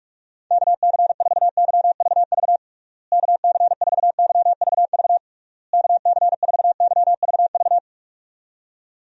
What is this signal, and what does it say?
Morse code, heard as a keyed tone, about 46 wpm, 700 Hz. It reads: KC4YVV KC4YVV KC4YVV
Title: USA Call Signs - 6 Characters - Repeated 3x - 45wpm